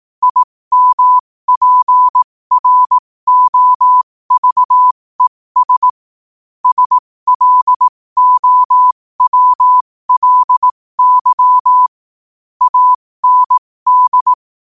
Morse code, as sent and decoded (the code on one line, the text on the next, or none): .. -- .--. .-. --- ...- . ... / ... .-.. --- .-- .-.. -.-- / .- -. -..
IMPROVES SLOWLY AND